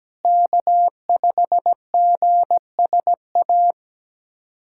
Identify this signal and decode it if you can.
Morse code, heard as a keyed tone, about 17 wpm, 700 Hz. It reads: K5GSA
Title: USA Callsign DE Callsign 17wpm